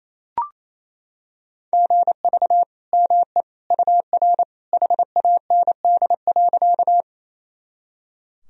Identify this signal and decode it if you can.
Morse code, heard as a keyed tone, about 28 wpm, 700 Hz. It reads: GV ME UR HAND.
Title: Sentences from Top 200 Words and Abbreviations 28wpm